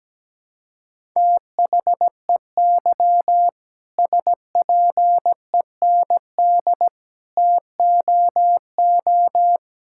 Morse code, heard as a keyed tone, about 17 wpm, 700 Hz. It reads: THEY SPEND TOO
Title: Sentences from Top 1000 Words - Review Part 5 - 17wpm